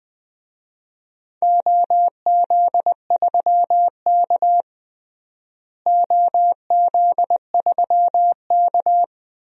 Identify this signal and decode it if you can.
Morse code, heard as a keyed tone, about 20 wpm, 700 Hz. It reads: OZ3K OZ3K
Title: DXCC-100 Call Signs - 4 Characters - Repeated 3x - 20wpm